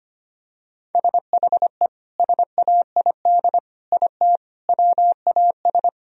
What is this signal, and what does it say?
Morse code, heard as a keyed tone, about 25 wpm, 700 Hz. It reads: SHE SAID IT WAS
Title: Sentences from Top 1000 Words - Review Part 1 - 25wpm